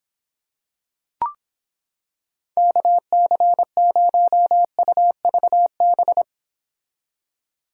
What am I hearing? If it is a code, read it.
Morse code, heard as a keyed tone, about 26 wpm, 700 Hz. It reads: KC0UVB